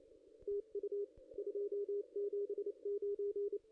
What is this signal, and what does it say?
Morse code, heard as a keyed tone, about 28 wpm, 410 Hz, with radio noise. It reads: TU 279
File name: contests / ss-cw-k5tr-op-n5rz -0400